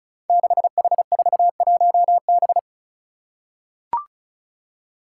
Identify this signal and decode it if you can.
Morse code, heard as a keyed tone, about 35 wpm, 700 Hz. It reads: 6H41B
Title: Five Challenging Letter-Number Combinations 35wpm